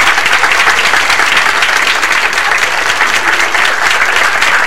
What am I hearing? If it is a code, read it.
Emergency Alert System (SAME) tones, heard as harsh buzzing bursts. It is heard from 0.9 to 3.5 s.